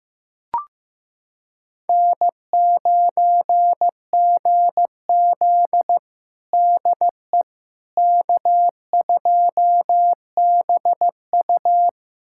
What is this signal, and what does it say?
Morse code, heard as a keyed tone, about 15 wpm, 700 Hz. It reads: N9GZ DE K2BU
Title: USA Callsign DE Callsign - 4 Characters - 15wpm